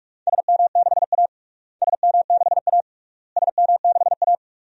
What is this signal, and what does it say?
Morse code, heard as a keyed tone, about 41 wpm, 700 Hz. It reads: SM6A SM6A SM6A
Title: DXCC-100 Call Signs - 4 Characters - Repeated 3x - 45wpm